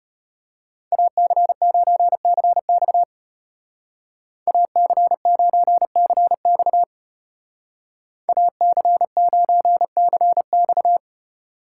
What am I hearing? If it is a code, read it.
Morse code, heard as a keyed tone, about 31 wpm, 700 Hz. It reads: AC9CX AC9CX AC9CX
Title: Speed Racing - USA 2x2 Call Signs 25wpm